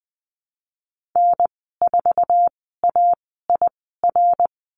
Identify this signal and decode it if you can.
Morse code, heard as a keyed tone, about 20 wpm, spaced as for 15 wpm, 700 Hz. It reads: N4AIR
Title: 055 Lesson 10g - Letter r - Call Signs 20wpm